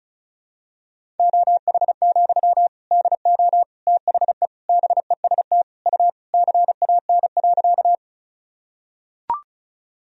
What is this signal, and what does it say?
Morse code, heard as a keyed tone, about 36 wpm, 700 Hz. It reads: OH, DO THE BEST U CAN.
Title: Sentences from Top 500 Words and Abbreviations 35wpm